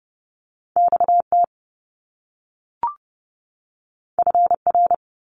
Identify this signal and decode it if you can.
Morse code, heard as a keyed tone, about 30 wpm, 700 Hz. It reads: XT FR
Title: Top Two Letter Combinations in Words 30wpm